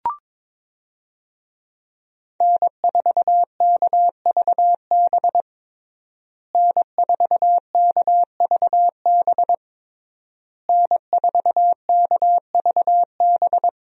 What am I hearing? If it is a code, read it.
Morse code, heard as a keyed tone, about 21 wpm, 700 Hz. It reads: N4KVB N4KVB N4KVB
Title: USA Call Signs - 5 Characters - Repeated 3x - 22wpm